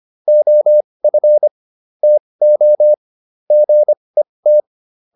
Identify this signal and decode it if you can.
Morse code, heard as a keyed tone, about 25 wpm, spaced as for 20 wpm, 600 Hz. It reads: OF TO GET
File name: WD204a 20